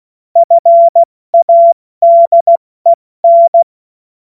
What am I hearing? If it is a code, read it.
Morse code, heard as a keyed tone, about 16 wpm, 680 Hz. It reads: FADEN
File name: Morsekurs / Witze / Witz 19 Dezember 2023 70 F80